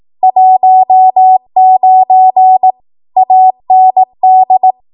Morse code, heard as a keyed tone, about 18 wpm, 750 Hz. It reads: 19 AND
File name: Archive / 18 WPM / 18WPM